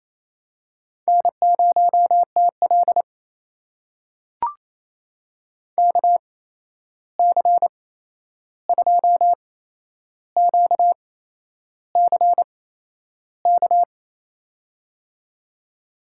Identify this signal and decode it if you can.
Morse code, heard as a keyed tone, about 28 wpm, spaced as for 11 wpm, 700 Hz. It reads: N0TL KC2QCK
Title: USA Call Signs - 8x Character Spacing - 28wpm